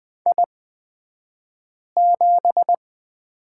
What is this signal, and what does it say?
Morse code, heard as a keyed tone, about 20 wpm, spaced as for 5 wpm, 700 Hz. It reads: I7
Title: 228 Lesson 37d - Letter x - Two Characters 20wpm